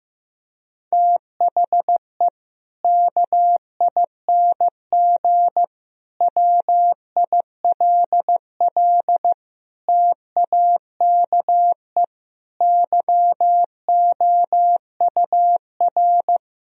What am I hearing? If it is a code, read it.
Morse code, heard as a keyed tone, about 16 wpm, 700 Hz. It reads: THE KING WILL TAKE YOUR
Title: Sentences from Top 1000 Words - Review Part 4 - 15wpm